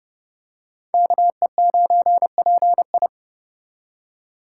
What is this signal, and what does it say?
Morse code, heard as a keyed tone, about 28 wpm, 700 Hz. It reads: KE9PI